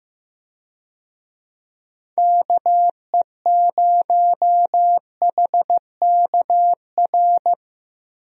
Morse code, heard as a keyed tone, about 16 wpm, 700 Hz. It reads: KE0HKR
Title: USA Call Signs - 6 Characters - Repeated 3x - 15wpm